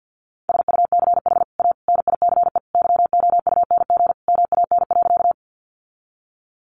Morse code, heard as a keyed tone, about 49 wpm, 700 Hz. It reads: SUCH A NICE YOUNG MAN.
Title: Sentences from Top 1100 Words 50wpm